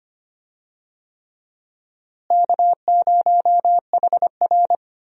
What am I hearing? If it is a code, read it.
Morse code, heard as a keyed tone, about 25 wpm, 700 Hz. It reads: K0HR
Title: USA Call Signs - 4 Characters - Repeated 3x - 25wpm